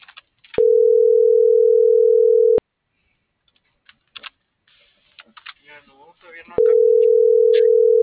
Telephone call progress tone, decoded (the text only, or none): ringback tone